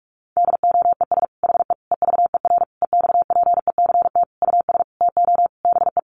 Morse code, heard as a keyed tone, about 44 wpm, 700 Hz. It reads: DOES HE EVER EXPECT US TO BE
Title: Sentences from Top 1100 Words 45wpm